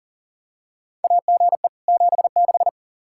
Morse code, heard as a keyed tone, about 40 wpm, 700 Hz. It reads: AGE 76
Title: Practice QSO - Exchange 3 - More Info - 40wpm